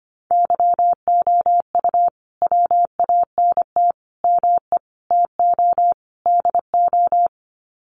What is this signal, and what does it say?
Morse code, heard as a keyed tone, about 25 wpm, 700 Hz. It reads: YOU WANT ME TO DO
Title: Sets of 5 Words Encore2 25wpm